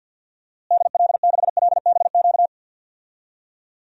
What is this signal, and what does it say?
Morse code, heard as a keyed tone, about 53 wpm, 700 Hz. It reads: DL6LBQ
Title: DXCC-100 Call Signs - 6 Characters - Repeated 3x - 50wpm